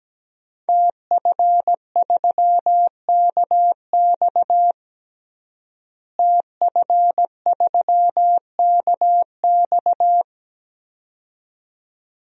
Morse code, heard as a keyed tone, about 17 wpm, 700 Hz. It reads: TF3KX TF3KX
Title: DXCC-100 Call Signs - 5 Characters - Repeated 2x - 17wpm